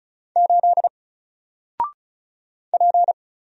Morse code, heard as a keyed tone, about 34 wpm, 700 Hz. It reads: 8 P
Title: Single Letter-Number 35wpm